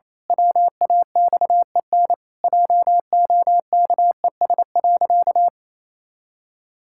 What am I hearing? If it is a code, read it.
Morse code, heard as a keyed tone, about 28 wpm, 700 Hz. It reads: WAXEN JOKES.
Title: Pangram Challenge v2 - 28wpm